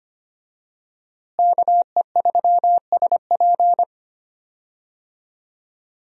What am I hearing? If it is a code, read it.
Morse code, heard as a keyed tone, about 25 wpm, 700 Hz. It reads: KE3SP
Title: USA Call Signs - 5 Characters - Repeated 3x - 25wpm